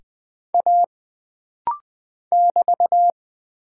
Morse code, heard as a keyed tone, about 20 wpm, 700 Hz. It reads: A <BT>
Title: Single Character (Letter:Number:Punctuation:Prosign) - Rapid-Fire - 20wpm